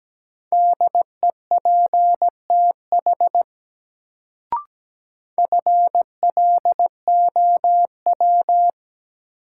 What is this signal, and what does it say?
Morse code, heard as a keyed tone, about 17 wpm, 700 Hz. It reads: DEPTH FLOW
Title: Sentences from Top 1200 Words - New Word Drill Alt - 17wpm